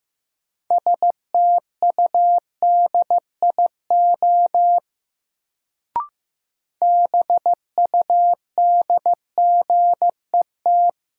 Morse code, heard as a keyed tone, about 15 wpm, 700 Hz. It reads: STUDIO BUDGET
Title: Sentences from Top 1200 Words - New Word Drill Alt - 15wpm